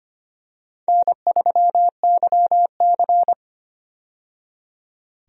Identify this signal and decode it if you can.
Morse code, heard as a keyed tone, about 25 wpm, 700 Hz. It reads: N3YC
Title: USA Call Signs - 4 Characters - 25wpm